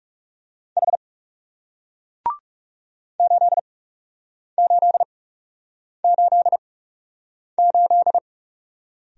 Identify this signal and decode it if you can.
Morse code, heard as a keyed tone, about 38 wpm, spaced as for 8 wpm, 700 Hz. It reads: H 8888